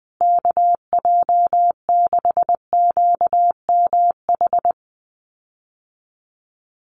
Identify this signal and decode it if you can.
Morse code, heard as a keyed tone, about 20 wpm, 700 Hz. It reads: KJ6QMH
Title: USA Call Signs Repeated 2x 20wpm